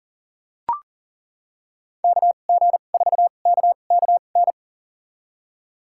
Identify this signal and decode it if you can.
Morse code, heard as a keyed tone, about 40 wpm, spaced as for 29 wpm, 700 Hz. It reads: KG4KKN
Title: USA Call Signs - 6 Characters - 2x Character Spacing - 40wpm